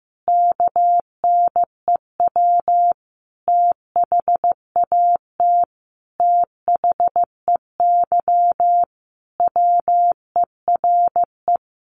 Morse code, heard as a keyed tone, about 15 wpm, 700 Hz. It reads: KNEW THAT THEY WERE
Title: Sentences from Top 1200 Words 15wpm